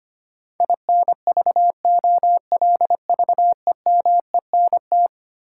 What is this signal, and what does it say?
Morse code, heard as a keyed tone, about 25 wpm, 700 Hz. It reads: INVOLVEMENT
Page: morse-code-ninja-practice-25wpm